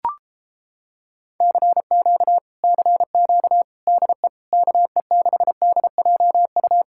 Morse code, heard as a keyed tone, about 29 wpm, 700 Hz. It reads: CQ CQ DE KE6DJU